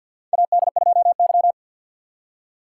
Morse code, heard as a keyed tone, about 47 wpm, 700 Hz. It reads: AD2Y
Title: USA Call Signs - 4 Characters - 2x Character Spacing - 50wpm